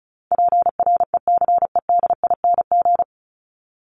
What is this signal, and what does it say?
Morse code, heard as a keyed tone, about 34 wpm, 700 Hz. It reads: PRECEDING
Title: ING Suffix - Top 100 Words with Base Words 35wpm